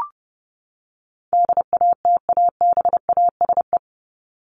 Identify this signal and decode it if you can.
Morse code, heard as a keyed tone, about 30 wpm, 700 Hz. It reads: DATABASE
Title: Top 1751-2000 Words 30wpm